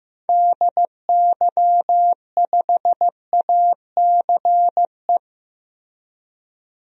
Morse code, heard as a keyed tone, about 15 wpm, 700 Hz. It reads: DY5ACE
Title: DXCC-100 Call Signs - 6 Characters - Repeated 2x - 15wpm